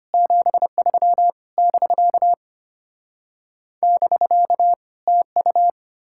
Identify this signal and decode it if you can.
Morse code, heard as a keyed tone, about 33 wpm, 700 Hz. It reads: 73 <BK> <BK> TU